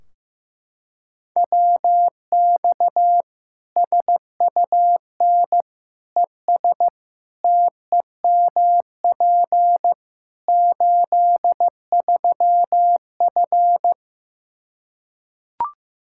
Morse code, heard as a keyed tone, about 15 wpm, 700 Hz. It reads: WX SUN ES TEMP 83F
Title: K1USN SST QSO Practice 15wpm